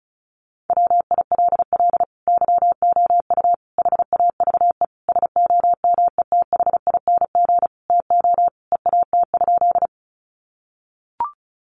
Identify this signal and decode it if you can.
Morse code, heard as a keyed tone, about 34 wpm, 700 Hz. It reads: WILL YOU HAVE SOMETHING TO EAT?
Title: Sentences from Top 600 Words 35wpm